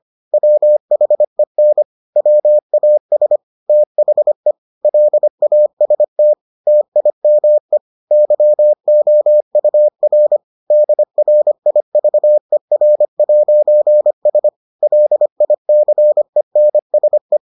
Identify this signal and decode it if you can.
Morse code, heard as a keyed tone, about 25 wpm, 600 Hz. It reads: WHEN WAS THE LAST TIME YOUR DRIVER'S LICENSE